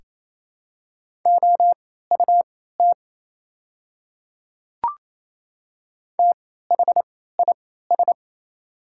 Morse code, heard as a keyed tone, about 28 wpm, spaced as for 16 wpm, 700 Hz. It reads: OUT THIS
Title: 3x Character Spacing - Top 200 Words - 28wpm